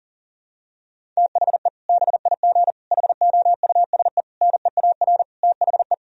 Morse code, heard as a keyed tone, about 40 wpm, 700 Hz. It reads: THE BIG HOUSE NEAR THE
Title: Sentences from Top 1000 Words - Review Part 1 - 40wpm